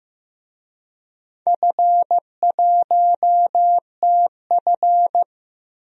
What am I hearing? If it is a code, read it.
Morse code, heard as a keyed tone, about 15 wpm, 700 Hz. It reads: F1TF
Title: DXCC-100 Call Signs - 4 Characters - Repeated 2x - 15wpm